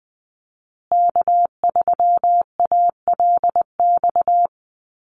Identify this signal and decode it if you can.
Morse code, heard as a keyed tone, about 20 wpm, 700 Hz. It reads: K3ALX
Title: USA Call Signs Repeated 3x 20wpm